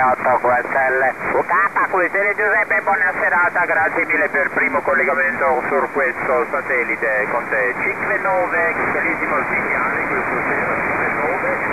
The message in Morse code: none